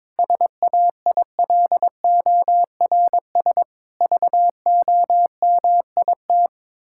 Morse code, heard as a keyed tone, about 22 wpm, 700 Hz. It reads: SAILORS VOMIT
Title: Pangram Challenge v2 - 22wpm